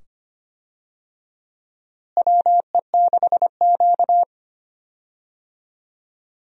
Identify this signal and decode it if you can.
Morse code, heard as a keyed tone, about 25 wpm, 700 Hz. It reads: WE6Q
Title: USA Call Signs - 4 Characters - Repeated 2x - 25wpm